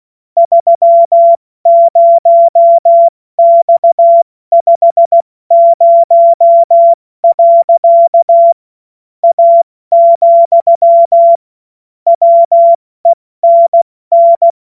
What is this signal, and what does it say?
Morse code, heard as a keyed tone, about 16 wpm, 680 Hz. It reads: 30X50. A, WENN
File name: Morsekurs / Witze / Witz 20. Januar 2026 70 F80